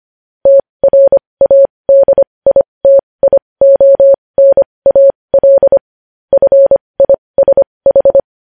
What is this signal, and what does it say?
Morse code, heard as a keyed tone, about 25 wpm, spaced as for 20 wpm, 550 Hz. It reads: TRADITIONAL FISH
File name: podcast / News headlines-1x-25x20